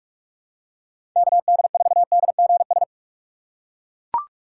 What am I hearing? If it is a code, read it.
Morse code, heard as a keyed tone, about 41 wpm, 700 Hz. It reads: KD4DGS
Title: USA Call Signs - 6 Characters - 2x Character Spacing - 45wpm